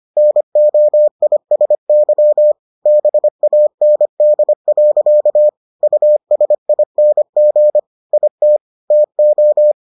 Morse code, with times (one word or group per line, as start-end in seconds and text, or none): 0.2-2.5 NOISY
2.9-5.5 BAND.
5.8-7.8 USING
8.1-8.6 IT
8.9-9.7 TO